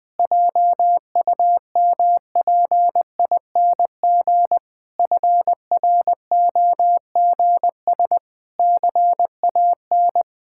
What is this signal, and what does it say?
Morse code, heard as a keyed tone, about 20 wpm, 700 Hz. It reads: JUMPING FROGS CAN
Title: Pangram Challenge v2 - 20wpm